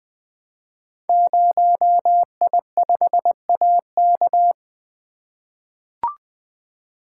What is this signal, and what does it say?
Morse code, heard as a keyed tone, about 20 wpm, 700 Hz. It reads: 0I5AK
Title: Code Groups - 5 Letter:Numbers - 20wpm